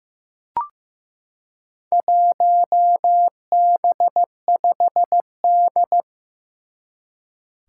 Morse code, heard as a keyed tone, about 15 wpm, 700 Hz. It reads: 1B5D
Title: Four Challenging Letter-Number Combinations 15wpm